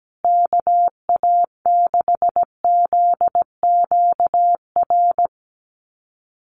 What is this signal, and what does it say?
Morse code, heard as a keyed tone, about 17 wpm, 700 Hz. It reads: KA6ZQR
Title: USA Call Signs Repeated 3x 17wpm